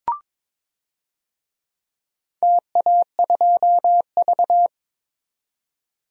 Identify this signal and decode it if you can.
Morse code, heard as a keyed tone, about 23 wpm, 700 Hz. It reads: TA2V
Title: DXCC-100 Call Signs - 4 Characters - 22wpm